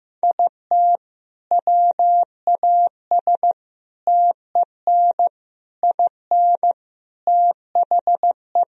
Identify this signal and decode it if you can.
Morse code, heard as a keyed tone, about 15 wpm, 700 Hz. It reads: IT WAS TEN IN THE
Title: Sentences from Top 1000 Words - Review Part 2 - 15wpm